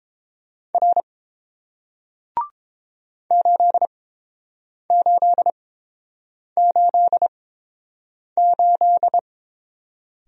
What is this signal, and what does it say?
Morse code, heard as a keyed tone, about 30 wpm, spaced as for 7 wpm, 700 Hz. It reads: R 8888